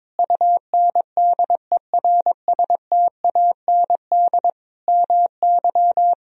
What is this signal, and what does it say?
Morse code, heard as a keyed tone, about 22 wpm, 700 Hz. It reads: UNDERSTAND MY
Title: Sentences from Top 1000 Words - Review Part 3 - 22wpm